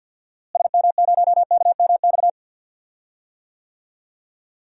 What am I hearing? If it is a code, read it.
Morse code, heard as a keyed tone, about 47 wpm, 700 Hz. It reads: SM0KMX
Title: DXCC-100 Call Signs - 6 Characters - Repeated 2x - 50wpm